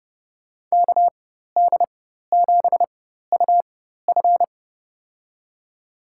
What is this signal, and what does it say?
Morse code, heard as a keyed tone, about 31 wpm, spaced as for 14 wpm, 700 Hz. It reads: KD7UF
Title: USA Call Signs - 5 Characters - 4x Character Spacing - 30wpm